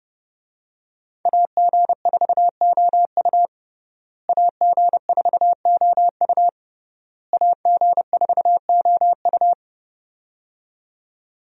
Morse code, heard as a keyed tone, about 30 wpm, 700 Hz. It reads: AG4OU AG4OU AG4OU
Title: USA Call Signs - 5 Characters - Repeated 3x - 30wpm